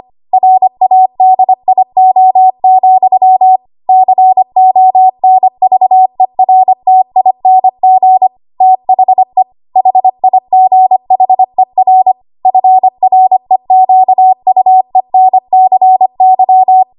Morse code, heard as a keyed tone, about 25 wpm, 750 Hz. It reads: RADIO, CONVERTING THE HIGHER FREQUENCY
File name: Archive / 25 WPM / 25WPM